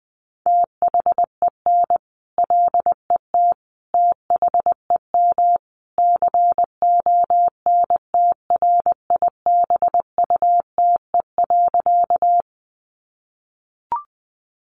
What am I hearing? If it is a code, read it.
Morse code, heard as a keyed tone, about 20 wpm, 700 Hz. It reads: THEN LET THEM CONTRIBUTE.
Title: Sentences from Top 1400 Words 20wpm